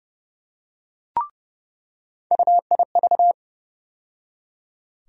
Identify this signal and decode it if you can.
Morse code, heard as a keyed tone, about 30 wpm, 700 Hz. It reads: UIV
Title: USA Call Sign Postfixes - 3 Letters 30wpm